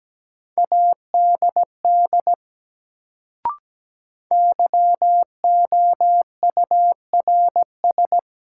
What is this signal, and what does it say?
Morse code, heard as a keyed tone, about 17 wpm, 700 Hz. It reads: ADD YOURS
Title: Sentences from Top 1500 Words - New Word Drill Alt - 17wpm